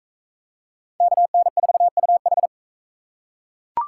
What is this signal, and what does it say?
Morse code, heard as a keyed tone, about 41 wpm, 700 Hz. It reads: KN4UH